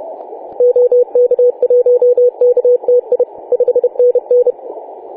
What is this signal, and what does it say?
Morse code, heard as a keyed tone, about 30 wpm, 490 Hz. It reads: OK1KTI 5NN
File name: 4W/G3ZEM 28.010MHz CW